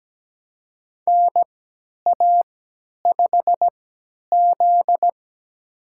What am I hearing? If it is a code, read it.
Morse code, heard as a keyed tone, about 17 wpm, spaced as for 10 wpm, 700 Hz. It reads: NA5Z
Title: USA Call Signs - 4 Characters - 3x Character Spacing - 17wpm